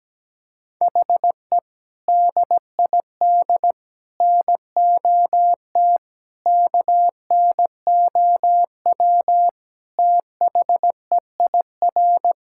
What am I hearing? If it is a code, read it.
Morse code, heard as a keyed tone, about 17 wpm, 700 Hz. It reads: HE DID NOT KNOW THEIR
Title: Sentences from Top 1000 Words - Review Part 6 - 17wpm